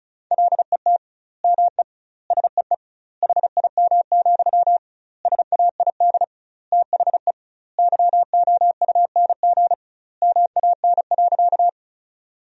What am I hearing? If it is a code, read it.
Morse code, heard as a keyed tone, about 36 wpm, 700 Hz. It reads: LET ME SEE HIM, SAID THE YOUNG MAN.
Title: 2x Word-Spacing - Sentences from Top 400 Words - 35wpm